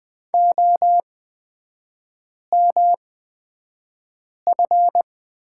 Morse code, heard as a keyed tone, about 20 wpm, spaced as for 5 wpm, 700 Hz. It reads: OMF
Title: 229 Lesson 37e - Letter x - Three Characters 20wpm